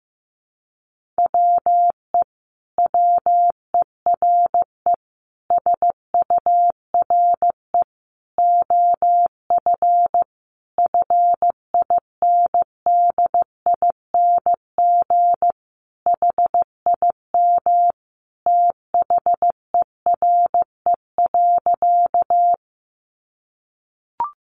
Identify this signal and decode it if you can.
Morse code, heard as a keyed tone, about 15 wpm, 700 Hz. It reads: WE WERE SURE OF FINDING HIM THERE.
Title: Sentences from Top 1100 Words 15wpm